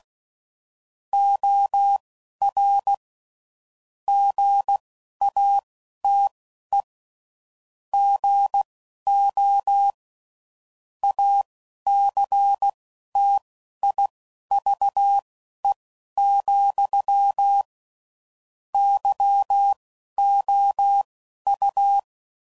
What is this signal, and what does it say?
Morse code, heard as a keyed tone, about 16 wpm, spaced as for 11 wpm, 800 Hz. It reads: OR GATE GO ACTIVE, YOU
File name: CW Code Groups @10WPM/Technical Subjects